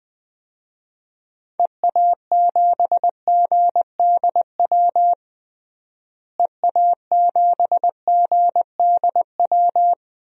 Morse code, heard as a keyed tone, about 20 wpm, 700 Hz. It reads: EA7GDW EA7GDW
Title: DXCC-100 Call Signs - 6 Characters - Repeated 2x - 20wpm